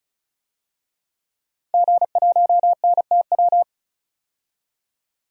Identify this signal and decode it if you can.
Morse code, heard as a keyed tone, about 36 wpm, 700 Hz. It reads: G1NTW
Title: DXCC-100 Call Signs - 5 Characters - 35wpm